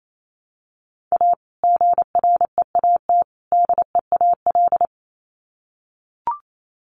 Morse code, heard as a keyed tone, about 28 wpm, 700 Hz. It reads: A GREAT DEAL
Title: Sets of 3 Words 28wpm